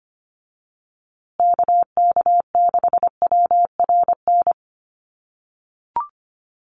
Morse code, heard as a keyed tone, about 25 wpm, 700 Hz. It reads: KK6WRN
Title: USA Call Signs Repeated 2x 25wpm